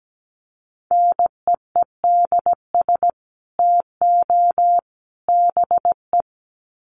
Morse code, heard as a keyed tone, about 17 wpm, 700 Hz. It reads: NEEDS TO BE
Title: Sets of 3 Words 17wpm